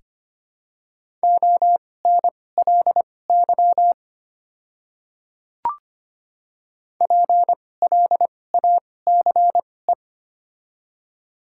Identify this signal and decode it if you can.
Morse code, heard as a keyed tone, about 25 wpm, spaced as for 18 wpm, 700 Hz. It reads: ONLY PLACE
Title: 2x Character Spacing - Top 200 Words - 25wpm